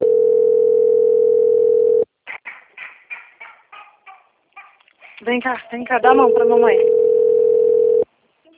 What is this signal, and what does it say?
Ringback tone (North America), 440 + 480 Hz, with speech.
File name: out- - - - - .14885